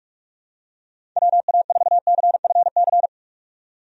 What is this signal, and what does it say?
Morse code, heard as a keyed tone, about 41 wpm, 700 Hz. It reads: WA4CFC
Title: USA Call Signs - 6 Characters - 45wpm